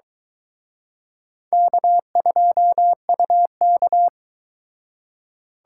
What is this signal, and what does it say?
Morse code, heard as a keyed tone, about 22 wpm, 700 Hz. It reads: K2UK